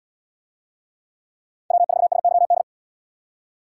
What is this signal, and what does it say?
Morse code, heard as a keyed tone, about 70 wpm, 700 Hz. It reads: W4SYD